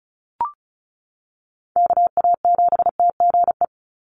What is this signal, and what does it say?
Morse code, heard as a keyed tone, about 34 wpm, 700 Hz. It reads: KA7TGE
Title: USA Call Signs 35wpm